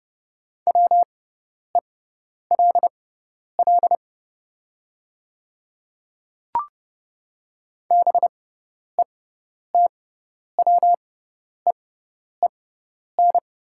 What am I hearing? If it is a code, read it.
Morse code, heard as a keyed tone, about 30 wpm, spaced as for 10 wpm, 700 Hz. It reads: WELL BETWEEN
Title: 6x Character Spacing - Top 300 Words - 30wpm